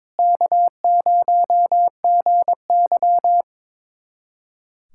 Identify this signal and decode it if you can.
Morse code, heard as a keyed tone, about 21 wpm, 700 Hz. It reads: K0GY
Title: USA Callsign DE Callsign - 4 Characters - 22wpm